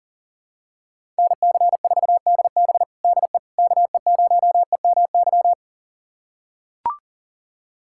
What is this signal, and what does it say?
Morse code, heard as a keyed tone, about 40 wpm, 700 Hz. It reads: NC4DB DE KE0EMY
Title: USA Callsign DE Callsign 40wpm